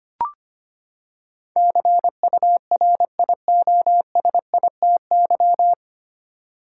Morse code, heard as a keyed tone, about 25 wpm, 700 Hz. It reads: CURIOSITY
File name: Nine Letter Words - 25wpm